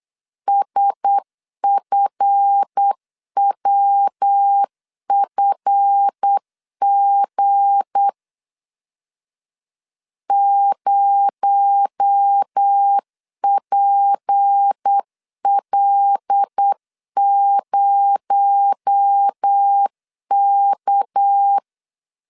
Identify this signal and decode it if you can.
Morse code, heard as a keyed tone, about 8 wpm, 800 Hz. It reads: SFWFG 0PL0K
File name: LEZIONI / 7 GF9Z / 7 GF 8wpm 3108